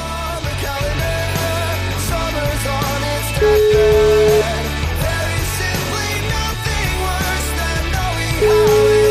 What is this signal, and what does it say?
Ringback tone (Europe, 425 Hz), with noise.